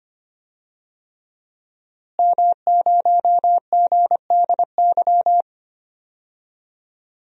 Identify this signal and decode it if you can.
Morse code, heard as a keyed tone, about 25 wpm, 700 Hz. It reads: M0GDY
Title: DXCC-100 Call Signs - 5 Characters - Repeated 3x - 25wpm